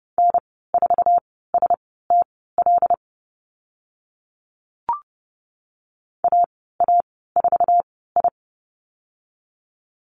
Morse code, heard as a keyed tone, about 30 wpm, spaced as for 17 wpm, 700 Hz. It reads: N4STL AA4I
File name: Lesson 14g - Number 2 - Call Signs 30wpm